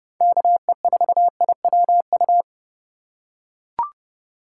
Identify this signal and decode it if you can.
Morse code, heard as a keyed tone, about 30 wpm, 700 Hz. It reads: KE4IWU
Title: USA Call Signs - 3x Character Spacing - 30wpm